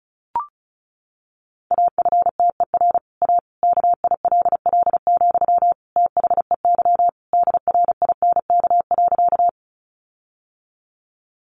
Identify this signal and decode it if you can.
Morse code, heard as a keyed tone, about 34 wpm, 700 Hz. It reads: AFTER A KILL, THEY DRINK.
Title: Sentences from Top 1500 Words 35wpm